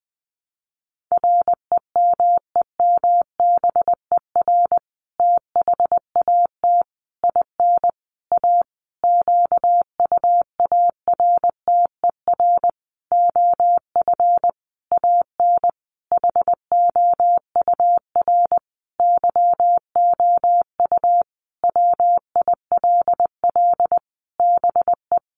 Morse code, heard as a keyed tone, about 20 wpm, 700 Hz. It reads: REMEMBER THAT IN A QUARTER OF AN HOUR YOU WILL BE
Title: Sentences from Top 1300 Words 20wpm